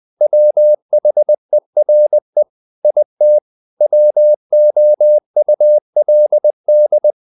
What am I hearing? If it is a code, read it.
Morse code, heard as a keyed tone, about 20 wpm, 600 Hz. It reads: WHERE IT WOULD
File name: the old man and the sea part 2 20wpm